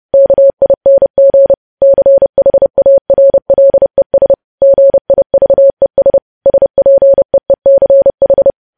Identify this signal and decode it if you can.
Morse code, heard as a keyed tone, about 30 wpm, 550 Hz. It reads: KING CHARLES GIVES SPEECH
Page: news headlines in morse code at 30 wpm